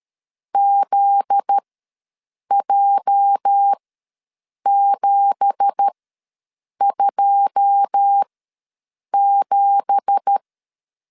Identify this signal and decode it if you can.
Morse code, heard as a keyed tone, about 13 wpm, spaced as for 7 wpm, 800 Hz. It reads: ZJ727
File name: LEZIONI / 9 2867 ws 12wpm 3431